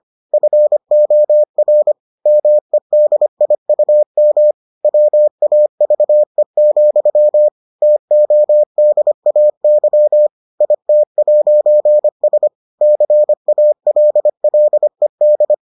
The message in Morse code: FOR MEDIUM WAVE, TODAY IT'S CALLED

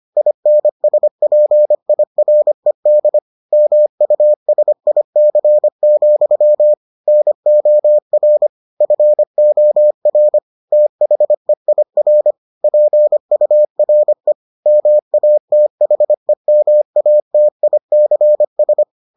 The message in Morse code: INSPIRED MUSIC, NOR FOR THEIR PURE MATHEMATICS

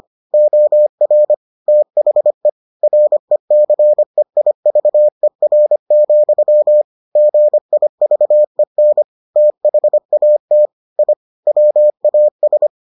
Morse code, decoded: OR THE RECEIVER, GIVEN THAT I WAS

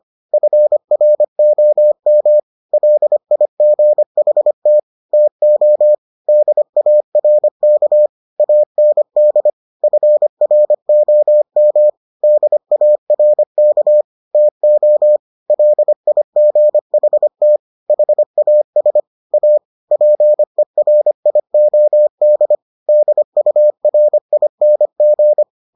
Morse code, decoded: FROM LIGHT TO DARK AND FROM DARK TO LIGHT HAS A PERIOD DURING

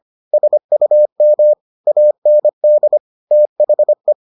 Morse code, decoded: SUM AND THE

